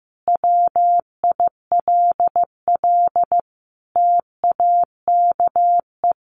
Morse code, heard as a keyed tone, about 15 wpm, 700 Hz. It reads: WILL TAKE
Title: Sentences from Top 1400 Words 15wpm